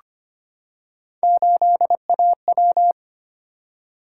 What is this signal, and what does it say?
Morse code, heard as a keyed tone, about 25 wpm, 700 Hz. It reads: 8AW
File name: Three Letter-Number Combinations - 25wpm